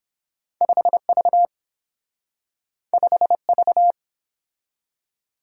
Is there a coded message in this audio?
Yes: Morse code from 0.6 to 3.9 s, heard as a keyed tone, about 30 wpm, 700 Hz. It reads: 5V 5V